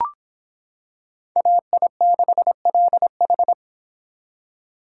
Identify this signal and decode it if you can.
Morse code, heard as a keyed tone, about 25 wpm, 700 Hz. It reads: AI6LH